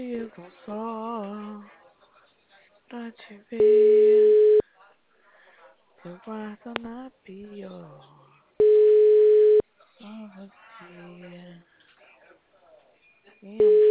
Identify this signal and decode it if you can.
Ringback tone (Europe, 425 Hz), with speech.